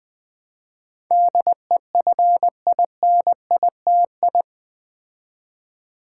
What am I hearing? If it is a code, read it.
Morse code, heard as a keyed tone, about 20 wpm, 700 Hz. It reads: DEFINITI
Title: Build A Word - 4 Letters At A Time - 10-Letter Words - 20wpm